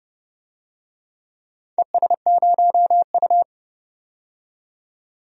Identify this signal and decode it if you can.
Morse code, heard as a keyed tone, about 30 wpm, 700 Hz. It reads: ES0U
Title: DXCC-100 Call Signs - 4 Characters - 30wpm